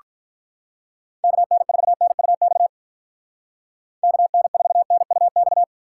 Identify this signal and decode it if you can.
Morse code, heard as a keyed tone, about 49 wpm, 700 Hz. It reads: KN4NUX KN4NUX